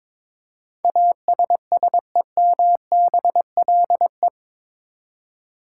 Morse code, heard as a keyed tone, about 22 wpm, 700 Hz. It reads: ASSEMBLE
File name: practice / 22wpm / Eight Letter Words - 22wpm